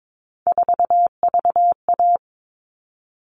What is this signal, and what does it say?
Morse code, heard as a keyed tone, about 22 wpm, 700 Hz. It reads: 4VA
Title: All Cut Numbers - 3-Digit 22wpm